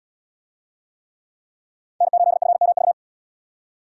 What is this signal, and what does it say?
Morse code, heard as a keyed tone, about 77 wpm, 700 Hz. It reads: N8FMV